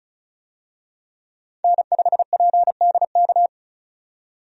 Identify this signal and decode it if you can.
Morse code, heard as a keyed tone, about 36 wpm, 700 Hz. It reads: N5PDK
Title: USA Call Signs - 5 Characters - 35wpm